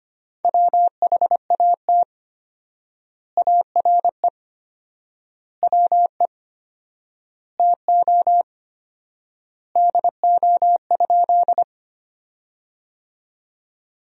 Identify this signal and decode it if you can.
Morse code, heard as a keyed tone, about 25 wpm, 700 Hz. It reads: WHAT ARE WE TO DO?
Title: 4x Word-Spacing - Sentences from Top 100 Words - 25wpm